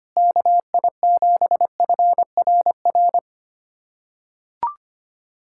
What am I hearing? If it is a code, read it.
Morse code, heard as a keyed tone, about 25 wpm, 700 Hz. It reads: KI7FRR
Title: USA Callsign DE Callsign 25wpm